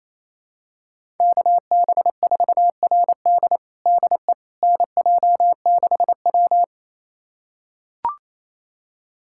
Morse code, heard as a keyed tone, about 28 wpm, 700 Hz. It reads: KB4RD DE NJ6W
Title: USA Callsign DE Callsign 28wpm